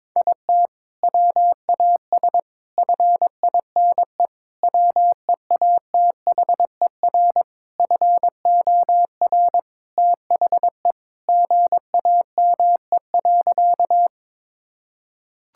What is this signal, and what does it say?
Morse code, heard as a keyed tone, about 21 wpm, 700 Hz. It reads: IT WAS FINE WEATHER FOR THE GAME.
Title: Sentences from Top 1000 Words - Review Part 5 - 22wpm